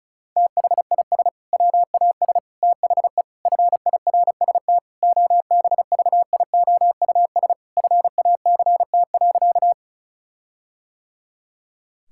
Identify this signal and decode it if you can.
Morse code, heard as a keyed tone, about 36 wpm, 700 Hz. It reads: THIS WAS THE FIRST OBVIOUS FACT.
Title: Sentences from Top 1600 Words 35wpm